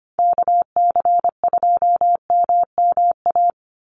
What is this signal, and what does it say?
Morse code, heard as a keyed tone, about 25 wpm, 700 Hz. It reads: KC2MMA
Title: USA Call Signs 25wpm